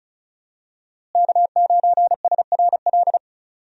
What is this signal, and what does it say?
Morse code, heard as a keyed tone, about 36 wpm, 700 Hz. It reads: K9SRL
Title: USA Call Signs - 5 Characters - Repeated 2x - 35wpm